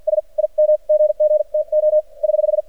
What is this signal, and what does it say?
Morse code, heard as a keyed tone, about 53 wpm, spaced as for 43 wpm, 610 Hz. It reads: S IM GG TO <HH>
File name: archived / CRSnet